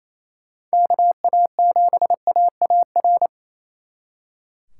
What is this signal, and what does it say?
Morse code, heard as a keyed tone, about 36 wpm, 700 Hz. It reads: KA7AAR